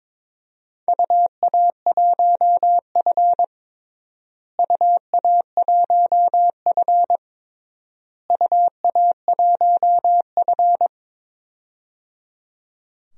Morse code, heard as a keyed tone, about 22 wpm, 700 Hz. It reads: UA1F UA1F UA1F
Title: DXCC-100 Call Signs - 4 Characters - Repeated 3x - 22wpm